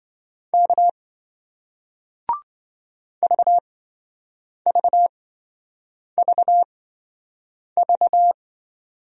Morse code, heard as a keyed tone, about 27 wpm, spaced as for 7 wpm, 700 Hz. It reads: K VVVV